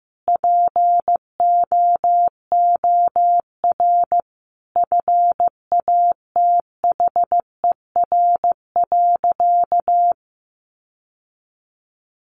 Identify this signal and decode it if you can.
Morse code, heard as a keyed tone, about 15 wpm, 700 Hz. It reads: POOR FATHER.
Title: Sentences from Top 1000 Words 15wpm